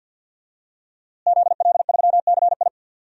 Morse code, heard as a keyed tone, about 51 wpm, 700 Hz. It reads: ZL3CI
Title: DXCC-100 Call Signs - 5 Characters - Repeated 2x - 50wpm